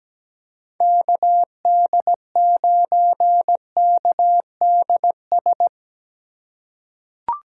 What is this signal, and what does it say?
Morse code, heard as a keyed tone, about 17 wpm, 700 Hz. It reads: KD9KDS
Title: USA Call Signs - 6 Characters - 8x Character Spacing - 17wpm